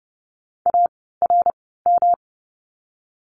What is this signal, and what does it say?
Morse code, heard as a keyed tone, about 30 wpm, spaced as for 17 wpm, 700 Hz. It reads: ARM
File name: course / 30:10wpm / 108 Lesson 18f - Letter m - Words 30wpm